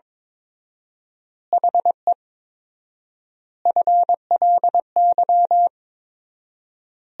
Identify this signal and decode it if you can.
Morse code, heard as a keyed tone, about 22 wpm, 700 Hz. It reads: HE FLY
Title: 4x Word-Spacing - Sentences from Top 500 Words - 22wpm